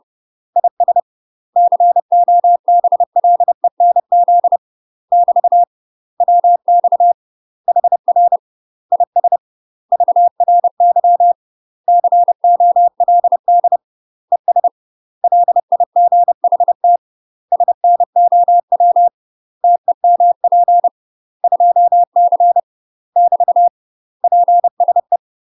Morse code, heard as a keyed tone, about 30 wpm, 700 Hz. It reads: IS COBLENZ <BT> WX HR IS VRY COLD ES LIGHT SNOW TEMP 2C <BT> PSE
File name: UK MASTER QSOs TRAINING/30WPM